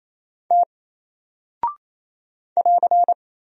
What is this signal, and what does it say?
Morse code, heard as a keyed tone, about 27 wpm, 700 Hz. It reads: T <AR>
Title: Single Character (Letter:Number:Punctuation:Prosign) Alt 28wpm